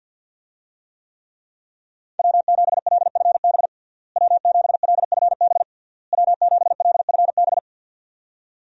Morse code, heard as a keyed tone, about 46 wpm, 700 Hz. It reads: W7LFB W7LFB W7LFB
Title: USA Call Signs - 5 Characters - Repeated 3x - 50wpm